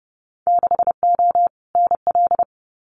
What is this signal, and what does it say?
Morse code, heard as a keyed tone, about 30 wpm, 700 Hz. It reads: 6O NL
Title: Winter Field Day Exchanges Encore 30wpm